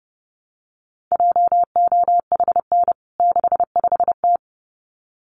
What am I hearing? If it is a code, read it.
Morse code, heard as a keyed tone, about 30 wpm, 700 Hz. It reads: JOHN 65T
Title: CWT Exchange 30wpm